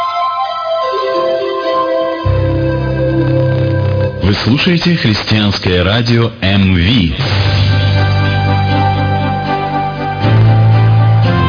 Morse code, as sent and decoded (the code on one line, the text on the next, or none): none